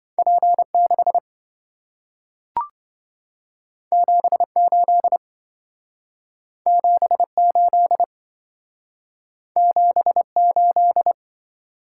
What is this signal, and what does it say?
Morse code, heard as a keyed tone, about 27 wpm, 700 Hz. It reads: P6 78 78 78